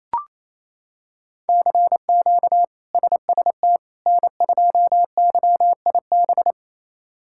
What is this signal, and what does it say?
Morse code, heard as a keyed tone, about 28 wpm, 700 Hz. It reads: CQ SST N2YIB
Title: K1USN SST QSO Practice - 28wpm